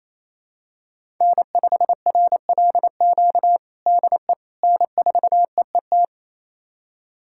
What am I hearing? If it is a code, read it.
Morse code, heard as a keyed tone, about 28 wpm, 700 Hz. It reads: N5RLQ DE N4EET
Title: USA Callsign DE Callsign - 5 Characters - 28wpm